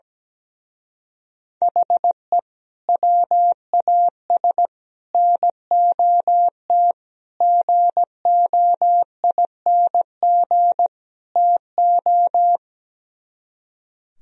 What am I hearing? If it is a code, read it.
Morse code, heard as a keyed tone, about 17 wpm, 700 Hz. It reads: HE WAS NOT GOING TO
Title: Speed-Racing - Sets of 5 Words 17wpm